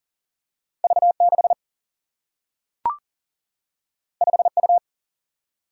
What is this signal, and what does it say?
Morse code, heard as a keyed tone, about 40 wpm, 700 Hz. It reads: V 6 5 U
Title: Single-Letter Single-Number 40wpm